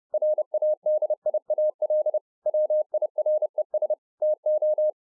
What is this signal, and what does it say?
Morse code, heard as a keyed tone, about 30 wpm, 610 Hz. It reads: RADIAL WIRES TO